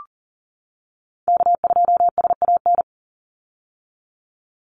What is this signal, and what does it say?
Morse code, heard as a keyed tone, about 38 wpm, 700 Hz. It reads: K2SAN
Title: Call Signs - 40wpm